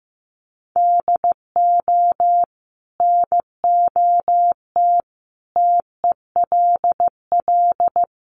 Morse code, heard as a keyed tone, about 15 wpm, 700 Hz. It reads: DO NOT TELL
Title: Sentences from Top 1500 Words 15wpm